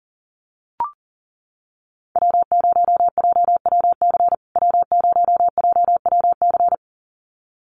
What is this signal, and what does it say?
Morse code, heard as a keyed tone, about 40 wpm, 700 Hz. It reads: W0JWC W0JWC
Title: USA Call Signs Repeated 2x 40wpm